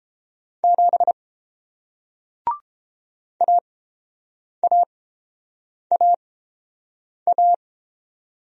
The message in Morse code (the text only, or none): 7 AAAA